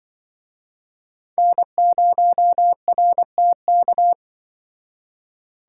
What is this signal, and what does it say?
Morse code, heard as a keyed tone, about 24 wpm, 700 Hz. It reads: N0RTK
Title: Speed Racing - USA 1x3 Call Signs 20wpm